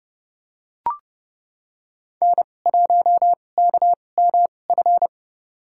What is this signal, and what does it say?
Morse code, heard as a keyed tone, about 30 wpm, spaced as for 22 wpm, 700 Hz. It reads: N1KMF
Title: USA Call Signs - 5 Characters - 2x Character Spacing - 30wpm